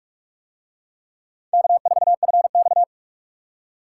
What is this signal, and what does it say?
Morse code, heard as a keyed tone, about 44 wpm, 700 Hz. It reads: K4FX